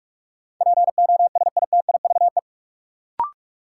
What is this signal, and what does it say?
Morse code, heard as a keyed tone, about 47 wpm, 700 Hz. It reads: POSITIVE
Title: Sentences from Top 1400 Words - New Word Drill Alt - 45wpm